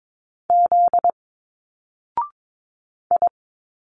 Morse code, heard as a keyed tone, about 22 wpm, 700 Hz. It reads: Z I
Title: Single Letters 22wpm